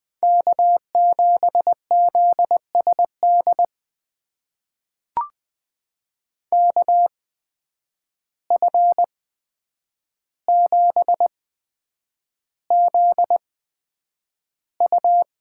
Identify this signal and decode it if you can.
Morse code, heard as a keyed tone, about 20 wpm, spaced as for 9 wpm, 700 Hz. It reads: K7ZSD KF7ZU
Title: USA Call Signs - 5 Characters - 8x Character Spacing - 20wpm